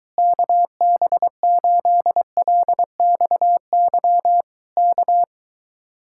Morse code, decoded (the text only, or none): KB8LXY K